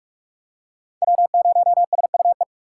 Morse code, heard as a keyed tone, about 44 wpm, 700 Hz. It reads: W0SUE